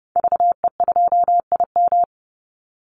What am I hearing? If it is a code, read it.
Morse code, heard as a keyed tone, about 30 wpm, 700 Hz. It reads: VE2IM
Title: Call Signs Repeated 2x - 30wpm